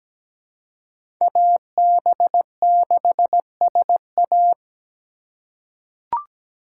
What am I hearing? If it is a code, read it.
Morse code, heard as a keyed tone, about 17 wpm, 700 Hz. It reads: AB6SA
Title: USA Call Signs - 5 Characters - 4x Character Spacing - 17wpm